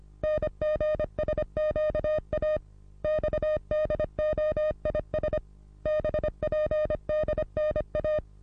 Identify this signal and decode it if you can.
Morse code, heard as a keyed tone, about 26 wpm, 610 Hz. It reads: NGSQA XDOIS BPDNA
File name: repos / Morsekurs / DJ4UF Morsen27